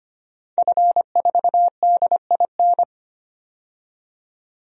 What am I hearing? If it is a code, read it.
Morse code, heard as a keyed tone, about 25 wpm, 700 Hz. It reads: F4DIN
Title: DXCC-100 Call Signs - 5 Characters - 25wpm